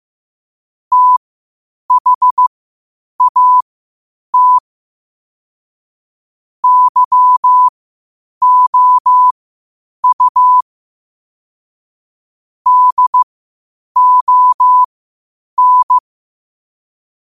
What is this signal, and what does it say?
Morse code, heard as a keyed tone, about 15 wpm, spaced as for 8 wpm, 1000 Hz. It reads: THAT YOU DON
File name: qotdfiles / 8wpm